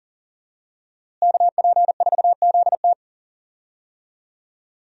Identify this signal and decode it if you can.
Morse code, heard as a keyed tone, about 40 wpm, 700 Hz. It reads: KP4ZT
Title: USA Call Signs - 5 Characters - Repeated 2x - 40wpm